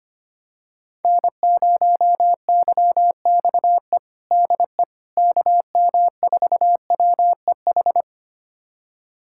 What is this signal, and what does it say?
Morse code, heard as a keyed tone, about 25 wpm, 700 Hz. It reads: N0YXE DE KM4WEH
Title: USA Callsign DE Callsign 25wpm